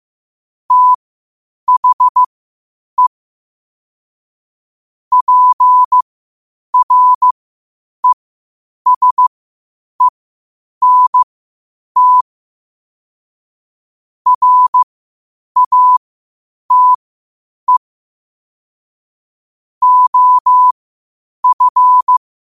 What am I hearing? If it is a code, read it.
Morse code, heard as a keyed tone, about 15 wpm, spaced as for 8 wpm, 1000 Hz. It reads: THE PRESENT RATE OF
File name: qotdfiles / 8wpm